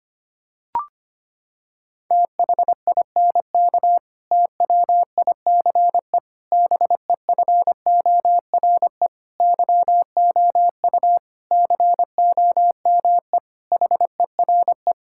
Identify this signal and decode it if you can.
Morse code, heard as a keyed tone, about 25 wpm, 700 Hz. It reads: THINK TWICE BEFORE YOU COME HERE
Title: Sentences from Top 1600 Words 25wpm